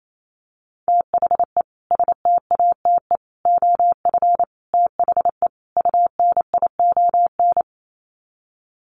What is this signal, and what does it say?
Morse code, heard as a keyed tone, about 28 wpm, 700 Hz. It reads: THE STATE OF THE UNION
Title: Sets of 5 Words Encore 28wpm